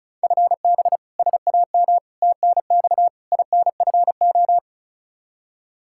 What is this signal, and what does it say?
Morse code, heard as a keyed tone, about 36 wpm, 700 Hz. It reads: FB SAM TNX INFO
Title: Practice QSO - Exchange 3 - More Info - 35wpm